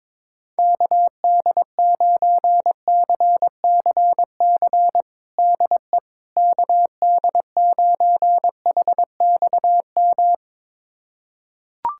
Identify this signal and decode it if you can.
Morse code, heard as a keyed tone, about 22 wpm, 700 Hz. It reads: KD9CCC DE KD9HXM
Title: USA Callsign DE Callsign - 6 Characters - 22wpm